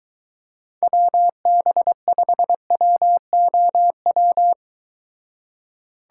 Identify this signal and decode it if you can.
Morse code, heard as a keyed tone, about 22 wpm, 700 Hz. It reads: WB5WOW